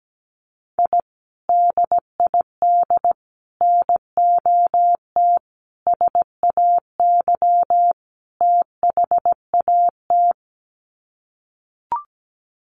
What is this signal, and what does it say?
Morse code, heard as a keyed tone, about 17 wpm, 700 Hz. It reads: I DID NOT SAY THAT
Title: Sets of 5 Words Encore2 17wpm